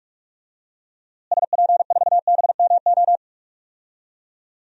Morse code, heard as a keyed tone, about 46 wpm, 700 Hz. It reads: SP4BMO
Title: DXCC-100 Call Signs - 6 Characters - Repeated 3x - 45wpm